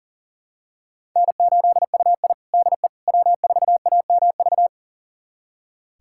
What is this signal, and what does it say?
Morse code, heard as a keyed tone, about 40 wpm, 700 Hz. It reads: N8UI DE W4AMV
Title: USA Callsign DE Callsign 40wpm